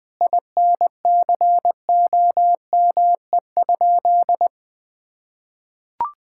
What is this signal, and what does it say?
Morse code, heard as a keyed tone, about 21 wpm, 700 Hz. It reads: INCOME?
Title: Sentences from Top 1000 Words - Review Part 5 - 20wpm